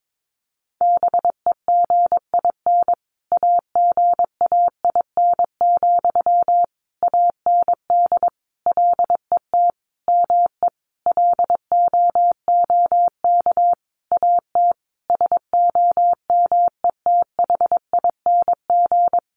Morse code, heard as a keyed tone, about 22 wpm, 700 Hz. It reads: BEGIN AGAIN, AND LET ME LOOK AT SOMETHING
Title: Sentences from Top 800 Words 22wpm